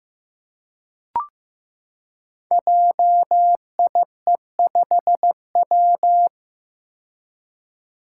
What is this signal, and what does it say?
Morse code, heard as a keyed tone, about 15 wpm, 700 Hz. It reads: JIE5W
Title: Code Groups - 5 Letter:Numbers - 15wpm